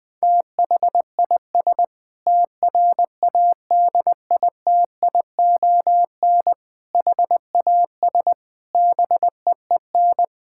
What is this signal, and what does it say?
Morse code, heard as a keyed tone, about 21 wpm, 700 Hz. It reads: THIS TRADITION HAS BEEN
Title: Sentences with 7-Word Phrases - 20wpm